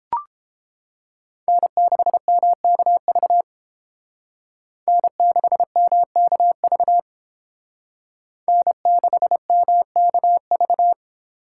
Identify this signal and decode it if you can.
Morse code, heard as a keyed tone, about 30 wpm, 700 Hz. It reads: N6MKV N6MKV N6MKV